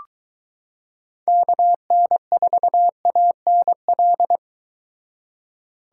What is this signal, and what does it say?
Morse code, heard as a keyed tone, about 22 wpm, 700 Hz. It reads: KN4ANL